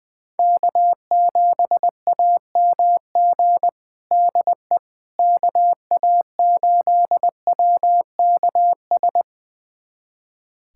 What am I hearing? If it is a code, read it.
Morse code, heard as a keyed tone, about 20 wpm, 700 Hz. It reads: K7AMG DE KA8WKS
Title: USA Callsign DE Callsign 20wpm